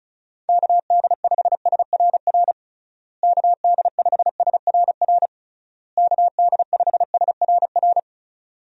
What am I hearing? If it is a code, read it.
Morse code, heard as a keyed tone, about 34 wpm, 700 Hz. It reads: KD5SRR KD5SRR KD5SRR
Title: USA Call Signs - 6 Characters - Repeated 3x - 35wpm